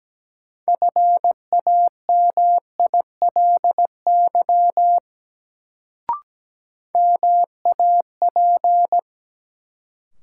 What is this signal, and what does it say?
Morse code, heard as a keyed tone, about 17 wpm, 700 Hz. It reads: FAMILY MAP
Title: Sentences from Top 400 Words - New Word Drill - 17wpm